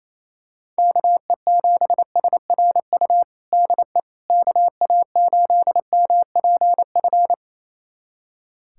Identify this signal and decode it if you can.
Morse code, heard as a keyed tone, about 28 wpm, 700 Hz. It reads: KE7SRU DE KA8MPF
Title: USA Callsign DE Callsign 28wpm